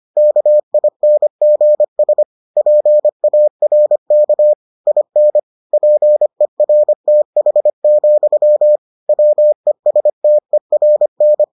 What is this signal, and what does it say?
Morse code, heard as a keyed tone, about 25 wpm, 600 Hz. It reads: KINGS PARK IN PERTH, WESTERN